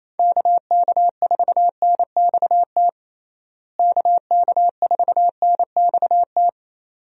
Morse code, heard as a keyed tone, about 27 wpm, 700 Hz. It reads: KK4NXT KK4NXT
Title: USA Call Signs - 6 Characters - Repeated 2x - 28wpm